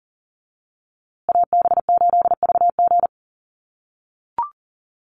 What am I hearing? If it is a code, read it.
Morse code, heard as a keyed tone, about 40 wpm, 700 Hz. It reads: AB8VG
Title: USA Call Signs Repeated 3x 40wpm